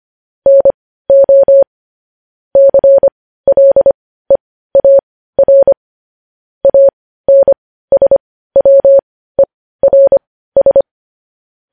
Morse code, heard as a keyed tone, about 25 wpm, spaced as for 15 wpm, 550 Hz. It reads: NO CLEAR ANSWERS